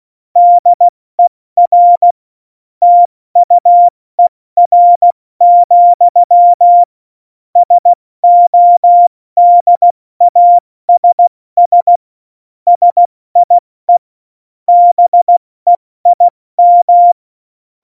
Morse code, heard as a keyed tone, about 16 wpm, 700 Hz. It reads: DER TUER, SODASS SIE BEIM
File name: Morsekurs / Witze / Witz 27. Oktober 2020, 70 F80